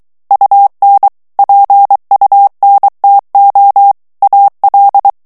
Morse code, heard as a keyed tone, about 23 wpm, 800 Hz. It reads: UN PUNTO AL
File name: ITALIEN texte clair 2 vitesse 22 mots minute